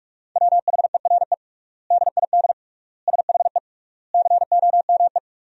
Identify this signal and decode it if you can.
Morse code, heard as a keyed tone, about 46 wpm, 700 Hz. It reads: WHERE DID SHE COME
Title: 3x Word-Spacing - Sentences from Top 200 Words - 45wpm